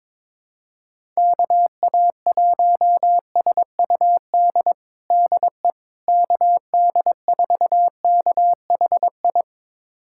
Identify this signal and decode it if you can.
Morse code, heard as a keyed tone, about 22 wpm, 700 Hz. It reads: KA1SUD DE KD4KHI
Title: USA Callsign DE Callsign - 6 Characters - 22wpm